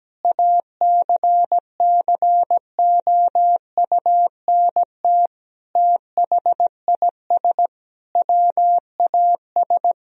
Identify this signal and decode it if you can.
Morse code, heard as a keyed tone, about 17 wpm, 700 Hz. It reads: ACCOUNT THIS WAS
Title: Sentences from Top 1000 Words - Review Part 1 - 17wpm